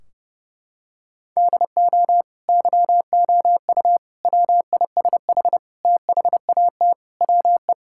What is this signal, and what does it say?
Morse code, heard as a keyed tone, about 30 wpm, 700 Hz. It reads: DO YOU WISH THAT WE
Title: Sentences from Top 1600 Words 30wpm